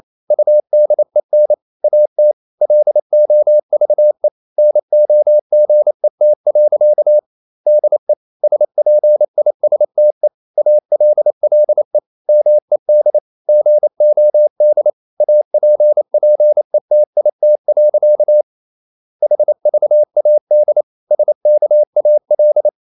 Morse code, heard as a keyed tone, about 28 wpm, 600 Hz. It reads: UDEN AT LOVE NOGET. DE SPISTE ALLE MED GOD APPETIT. HVAD SKAL
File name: QRQ 28wpm